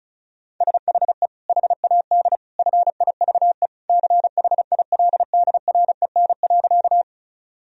Morse code, heard as a keyed tone, about 34 wpm, 700 Hz. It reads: SHE HAD FIVE CHILDREN.
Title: Sentences from Top 1000 Words - Review Part 2 - 35wpm